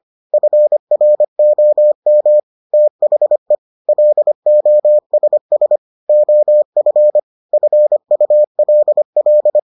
Morse code, heard as a keyed tone, about 25 wpm, 600 Hz. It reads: FROM THE LOSS OF FULL